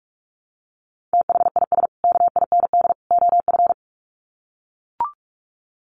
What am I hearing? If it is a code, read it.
Morse code, heard as a keyed tone, about 44 wpm, 700 Hz. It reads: THIS KIND OF
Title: Sets of 3 Words 45wpm